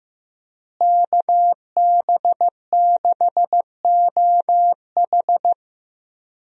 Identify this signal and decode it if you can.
Morse code, heard as a keyed tone, about 16 wpm, 700 Hz. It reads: KB6OH
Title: USA Call Signs - 5 Characters - Repeated 2x - 15wpm